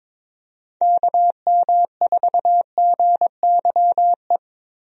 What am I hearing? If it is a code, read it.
Morse code, heard as a keyed tone, about 22 wpm, 700 Hz. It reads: KM4GYE
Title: USA Call Signs - 6 Characters - Repeated 3x - 22wpm